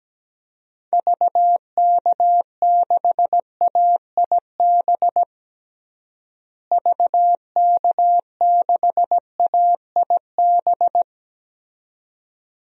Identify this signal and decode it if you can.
Morse code, heard as a keyed tone, about 17 wpm, 700 Hz. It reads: VK6AIB VK6AIB
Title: DXCC-100 Call Signs - 6 Characters - Repeated 2x - 17wpm